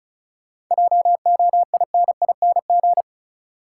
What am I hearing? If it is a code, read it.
Morse code, heard as a keyed tone, about 36 wpm, 700 Hz. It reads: JOINING
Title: Sets of 5 Words - New Word Drill - 35wpm